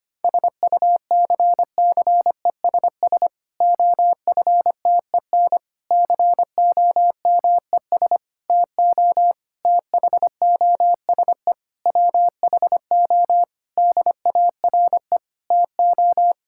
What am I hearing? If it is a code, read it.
Morse code, heard as a keyed tone, about 25 wpm, 700 Hz. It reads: SUCCESS OFTEN COMES TO THOSE WHO DARE TO
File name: practice / 25wpm / Difficult Sentences with Hints - 25wpm